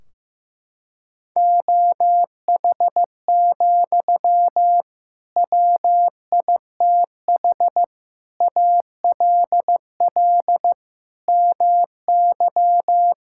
Morse code, heard as a keyed tone, about 15 wpm, 700 Hz. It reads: OH, WITH ALL MY
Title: Sentences from Top 1600 Words 15wpm